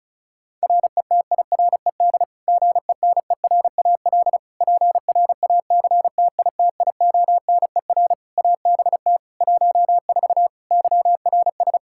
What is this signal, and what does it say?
Morse code, heard as a keyed tone, about 36 wpm, 700 Hz. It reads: RETIRED GENERAL PRACTITIONER ABT 14 YRS
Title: Practice QSO - Exchange 3 - More Info - 35wpm